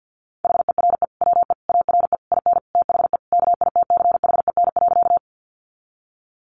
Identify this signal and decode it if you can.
Morse code, heard as a keyed tone, about 50 wpm, 700 Hz. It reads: HERE WE ARE IN THE KITCHEN.
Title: Sentences from Top 1400 Words 50wpm